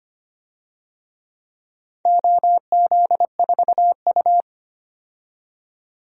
Morse code, heard as a keyed tone, about 25 wpm, 700 Hz. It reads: OZ4U
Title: DXCC-100 Call Signs - 4 Characters - 25wpm